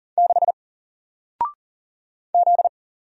Morse code, heard as a keyed tone, about 39 wpm, 700 Hz. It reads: 6 Z
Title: Single Character (Letter:Number:Punctuation:Prosign) Alt 40wpm